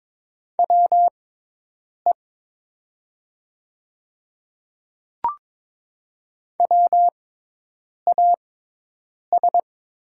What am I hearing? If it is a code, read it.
Morse code, heard as a keyed tone, about 22 wpm, spaced as for 8 wpm, 700 Hz. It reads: WE WAS
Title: 6x Character Spacing - Top 100 Words - 22wpm